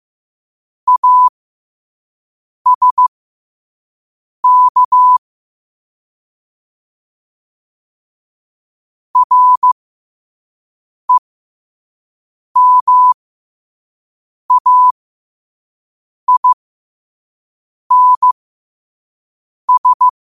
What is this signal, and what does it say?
Morse code, heard as a keyed tone, about 15 wpm, spaced as for 5 wpm, 1000 Hz. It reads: ASK REMAINS
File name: qotdfiles / 5wpm